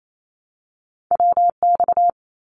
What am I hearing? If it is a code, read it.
Morse code, heard as a keyed tone, about 28 wpm, 700 Hz. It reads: WX
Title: Abbreviations Repeated 3x - 28wpm